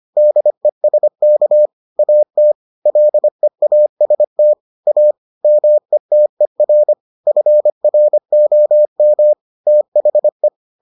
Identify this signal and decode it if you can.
Morse code, heard as a keyed tone, about 25 wpm, 600 Hz. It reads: DESK AT LEAST A METER FROM THE